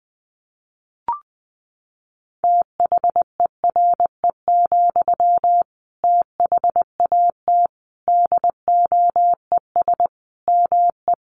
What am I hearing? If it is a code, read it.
Morse code, heard as a keyed tone, about 20 wpm, 700 Hz. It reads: THERE, THAT DOES ME
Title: Sentences from Top 200 Words 20wpm